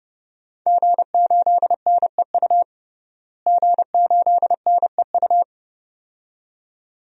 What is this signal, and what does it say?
Morse code, heard as a keyed tone, about 30 wpm, 700 Hz. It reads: G8NEU G8NEU
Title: DXCC-100 Call Signs - 5 Characters - Repeated 2x - 30wpm